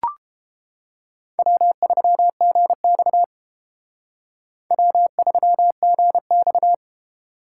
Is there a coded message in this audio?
Yes: Morse code from 1.4 to 6.7 s, heard as a keyed tone, about 28 wpm, 700 Hz. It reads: W3GX W3GX